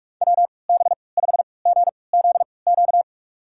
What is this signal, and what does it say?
Morse code, heard as a keyed tone, about 46 wpm, spaced as for 25 wpm, 700 Hz. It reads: WB5GZQ
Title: USA Call Signs - 6 Characters - 3x Character Spacing - 45wpm